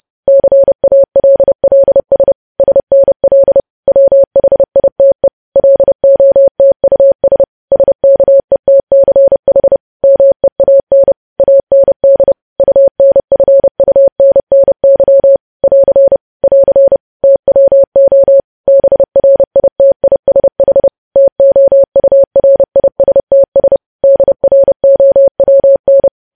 Morse code, heard as a keyed tone, about 30 wpm, 550 Hz. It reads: CALLS SNL WHITE LOTUS SKETCH MEAN AND UNFUNNY <AR> <AR> TWO BRITISH TOURISTS DROWN
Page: news headlines in morse code at 30 wpm